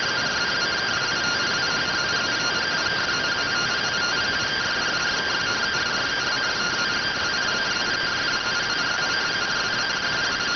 Two-tone teleprinter signal, about 170 Hz shift.